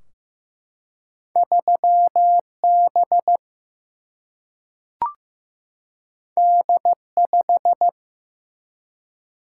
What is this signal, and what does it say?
Morse code, heard as a keyed tone, about 15 wpm, 700 Hz. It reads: 3 B D 5
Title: Single-Letter Single-Number 15wpm